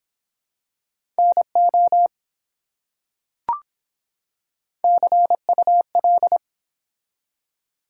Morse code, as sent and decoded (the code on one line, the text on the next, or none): -. --- / -.-. ..- .-..
NO CUL